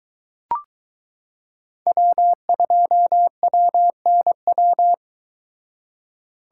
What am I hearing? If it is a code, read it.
Morse code, heard as a keyed tone, about 23 wpm, 700 Hz. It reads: W2WNW